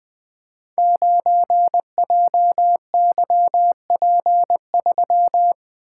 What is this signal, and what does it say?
Morse code, heard as a keyed tone, about 20 wpm, 700 Hz. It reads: 9JYP3
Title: Code Groups - 5 Letter:Numbers - 20wpm